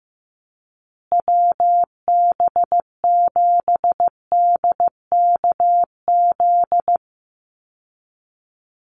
Morse code, heard as a keyed tone, about 15 wpm, 700 Hz. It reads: WB7DKZ
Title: Call Signs Repeated 3x - 15wpm